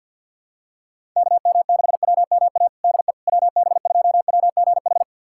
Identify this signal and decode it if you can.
Morse code, heard as a keyed tone, about 51 wpm, 700 Hz. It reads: KM6WMA DE WB2WGH
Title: USA Callsign DE Callsign - 6 Characters - 50wpm